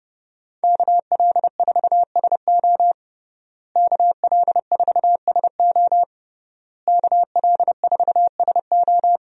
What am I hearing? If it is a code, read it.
Morse code, heard as a keyed tone, about 32 wpm, 700 Hz. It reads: KL4SO KL4SO KL4SO
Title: USA Call Signs - 5 Characters - Repeated 3x - 30wpm